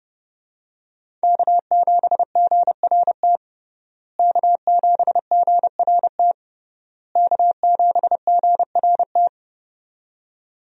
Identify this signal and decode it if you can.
Morse code, heard as a keyed tone, about 32 wpm, 700 Hz. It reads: K7GRT K7GRT K7GRT
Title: USA Call Signs - 5 Characters - Repeated 3x - 30wpm